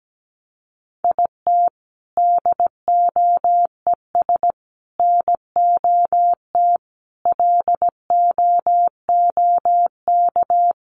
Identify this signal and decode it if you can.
Morse code, heard as a keyed tone, about 17 wpm, 700 Hz. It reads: IT DOES NOT LOOK
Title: Sets of 5 Words Encore2 17wpm